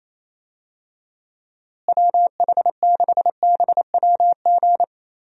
Morse code, heard as a keyed tone, about 27 wpm, 700 Hz. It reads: WH6BWG
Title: USA Call Signs - 6 Characters - Repeated 3x - 28wpm